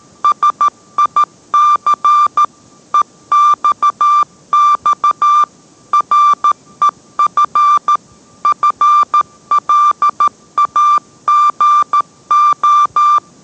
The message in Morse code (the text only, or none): SIC EXX REF FLAGO